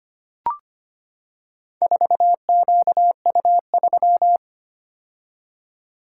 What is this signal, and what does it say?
Morse code, heard as a keyed tone, about 25 wpm, 700 Hz. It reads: 4QU3
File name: practice / 25wpm / Four Letter-Number Combinations - 25wpm